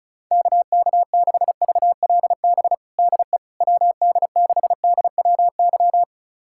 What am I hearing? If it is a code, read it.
Morse code, heard as a keyed tone, about 36 wpm, 700 Hz. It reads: KK6VLB DE WD6DWY
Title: USA Callsign DE Callsign - 6 Characters - 35wpm